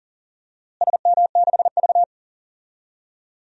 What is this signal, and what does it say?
Morse code, heard as a keyed tone, about 40 wpm, 700 Hz. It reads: SM6V
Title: DXCC-100 Call Signs - 4 Characters - Repeated 3x - 40wpm